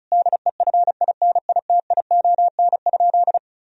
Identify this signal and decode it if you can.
Morse code, heard as a keyed tone, about 34 wpm, 700 Hz. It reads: DEFINITION?
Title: Sentences from Top 1000 Words - Review Part 4 - 35wpm